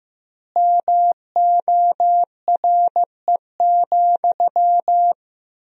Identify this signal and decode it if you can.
Morse code, heard as a keyed tone, about 16 wpm, 700 Hz. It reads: MORE,
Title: 2x Word-Spacing - Sentences from Top 300 Words - 15wpm